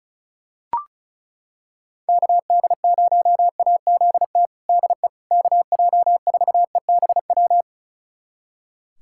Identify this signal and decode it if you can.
Morse code, heard as a keyed tone, about 36 wpm, 700 Hz. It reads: KD0AZT DE KJ4EBW
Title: USA Callsign DE Callsign 35wpm